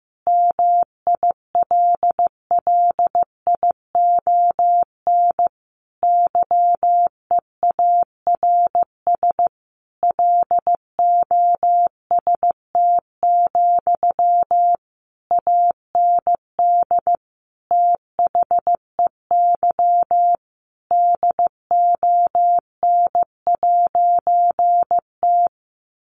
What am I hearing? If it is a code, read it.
Morse code, heard as a keyed tone, about 15 wpm, 700 Hz. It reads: MILLION YEARS LOST, AND THEY DON'T
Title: A Sound Of Thunder 15wpm